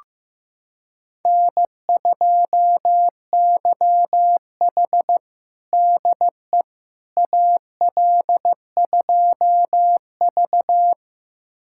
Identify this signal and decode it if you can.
Morse code, heard as a keyed tone, about 15 wpm, 700 Hz. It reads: N2YH DE AL2V
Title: USA Callsign DE Callsign - 4 Characters - 15wpm